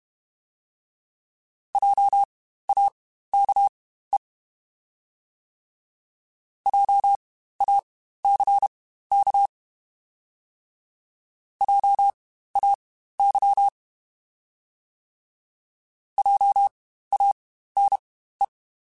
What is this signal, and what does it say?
Morse code, heard as a keyed tone, about 32 wpm, spaced as for 15 wpm, 800 Hz. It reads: JAKE JACK JAY JANE
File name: Spaced Alphas / Spaced Alphas mp3s / Js sp @30